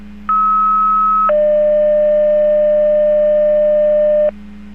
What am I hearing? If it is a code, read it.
Two-tone page, heard as a one-second tone, then a three-second tone, about 1290 then 600 Hz. It is heard from 0.3 to 4.3 s.